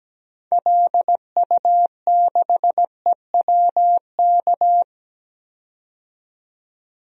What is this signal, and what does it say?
Morse code, heard as a keyed tone, about 17 wpm, 700 Hz. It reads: LU6EWK
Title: DXCC-100 Call Signs - 6 Characters - 17wpm